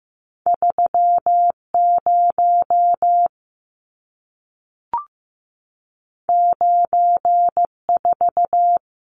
Morse code, heard as a keyed tone, about 15 wpm, 700 Hz. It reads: 30 94
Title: Numbers - 2-Digit 15wpm